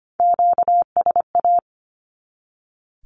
Morse code, heard as a keyed tone, about 25 wpm, 700 Hz. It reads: QSA
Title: Q-Codes 25wpm